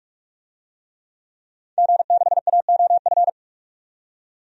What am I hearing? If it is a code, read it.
Morse code, heard as a keyed tone, about 46 wpm, 700 Hz. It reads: G6AOF
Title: DXCC-100 Call Signs - 5 Characters - 45wpm